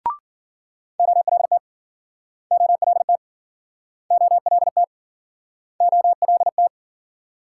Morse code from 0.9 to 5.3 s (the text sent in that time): OLT OLT OLT